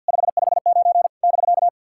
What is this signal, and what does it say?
Morse code, heard as a keyed tone, about 51 wpm, 690 Hz. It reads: 559 <BK>
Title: SOTA QSO Practice 50wpm